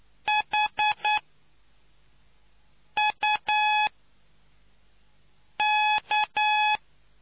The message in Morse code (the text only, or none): HUK